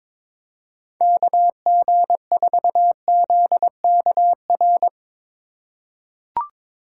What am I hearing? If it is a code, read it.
Morse code, heard as a keyed tone, about 22 wpm, 700 Hz. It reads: KG4ZKR
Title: USA Call Signs - 6 Characters - 4x Character Spacing - 22wpm